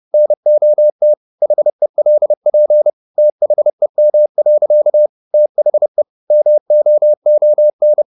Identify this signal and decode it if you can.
Morse code, heard as a keyed tone, about 30 wpm, 600 Hz. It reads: NOT HELP THEM. THE MOON
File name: the old man and the sea / the old man and the sea 1 30wpm